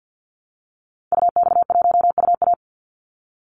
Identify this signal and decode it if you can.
Morse code, heard as a keyed tone, about 50 wpm, 700 Hz. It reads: UX1UA
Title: Call Signs Repeated 3x - 50wpm